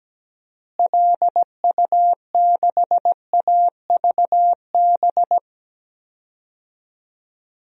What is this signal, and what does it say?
Morse code, heard as a keyed tone, about 17 wpm, 700 Hz. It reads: LU6AVB
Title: DXCC-100 Call Signs - 6 Characters - Repeated 3x - 17wpm